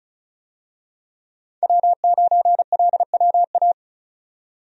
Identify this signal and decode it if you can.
Morse code, heard as a keyed tone, about 34 wpm, 700 Hz. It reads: W9LWA
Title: USA Call Signs - 5 Characters - 35wpm